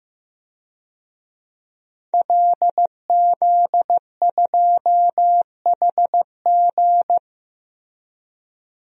Morse code, heard as a keyed tone, about 15 wpm, 700 Hz. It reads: LZ2HG
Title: DXCC-100 Call Signs - 5 Characters - 15wpm